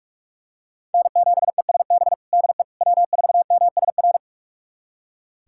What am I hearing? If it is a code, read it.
Morse code, heard as a keyed tone, about 42 wpm, 700 Hz. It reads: N7ESB DE W4MSR
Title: USA Callsign DE Callsign - 5 Characters - 45wpm